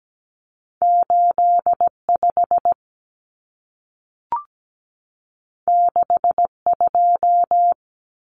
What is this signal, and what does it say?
Morse code, heard as a keyed tone, about 17 wpm, 700 Hz. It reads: 85 62
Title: Numbers - 2-Digit 17wpm